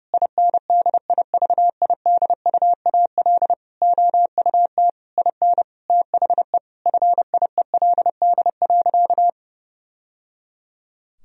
Morse code, heard as a keyed tone, about 30 wpm, 700 Hz. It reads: INDIVIDUAL OUT IN THE FIELD.
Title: Sentences from Top 1000 Words - Review Part 4 - 30wpm